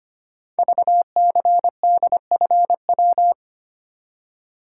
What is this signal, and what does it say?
Morse code, heard as a keyed tone, about 25 wpm, 700 Hz. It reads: VCDFW
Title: Code Groups - 5 Letters - 25wpm